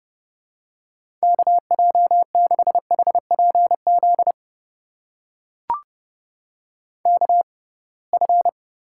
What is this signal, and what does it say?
Morse code, heard as a keyed tone, about 31 wpm, spaced as for 23 wpm, 700 Hz. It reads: KJ6HPZ KF
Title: USA Call Signs - 6 Characters - 6x Character Spacing - 30wpm